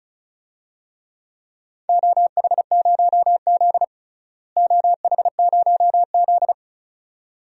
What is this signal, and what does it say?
Morse code, heard as a keyed tone, about 36 wpm, 700 Hz. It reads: OH0Z OH0Z
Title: DXCC-100 Call Signs - 4 Characters - Repeated 2x - 35wpm